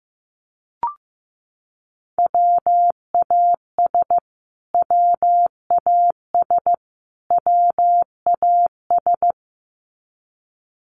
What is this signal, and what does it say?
Morse code, heard as a keyed tone, about 14 wpm, 700 Hz. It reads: WAS WAS WAS
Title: Top 500 Words Repeated 15wpm